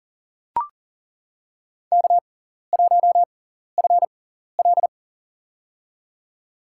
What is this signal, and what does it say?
Morse code, heard as a keyed tone, about 37 wpm, spaced as for 14 wpm, 700 Hz. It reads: K1FL
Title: USA Call Signs - 4 Characters - 6x Character Spacing - 40wpm